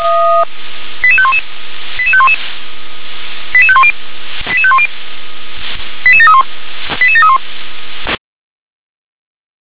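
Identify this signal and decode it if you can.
Five-tone selective call (ZVEI), heard as quick runs of five tones, tones of about 70 ms. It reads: 88511 88511 88510 88510 88731 88731
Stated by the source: Five-tone ZVEI selective call received on SDR.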